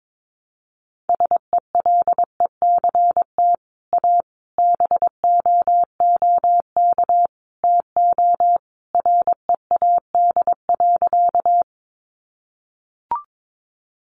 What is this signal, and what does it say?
Morse code, heard as a keyed tone, about 22 wpm, 700 Hz. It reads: SELECT A BOOK TO READ.
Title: Sentences from Top 1300 Words 22wpm